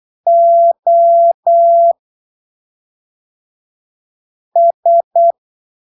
Morse code, heard as a keyed tone, about 8 wpm, spaced as for 3 wpm, 680 Hz. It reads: OS
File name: arquivos / texto / texto 08 08ppm